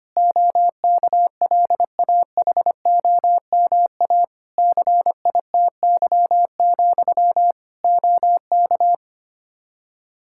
Morse code, heard as a keyed tone, about 25 wpm, 700 Hz. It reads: OKLAHOMA CITY, OK
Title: Practice QSO - Exchange 1 - Beginning - 25wpm